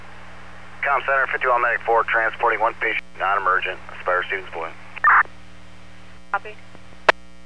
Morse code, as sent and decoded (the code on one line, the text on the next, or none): none